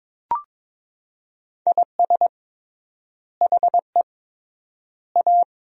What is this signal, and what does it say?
Morse code, heard as a keyed tone, about 22 wpm, 700 Hz. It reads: IS HE A
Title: 3x Word-Spacing - Sentences from Top 300 Words - 22wpm